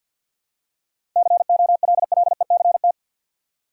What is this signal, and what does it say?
Morse code, heard as a keyed tone, about 50 wpm, 700 Hz. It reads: COLLECT
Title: Seven Letter Words - 50wpm